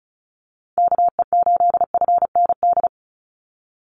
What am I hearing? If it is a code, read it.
Morse code, heard as a keyed tone, about 34 wpm, 700 Hz. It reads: KE8FND
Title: USA Call Signs 35wpm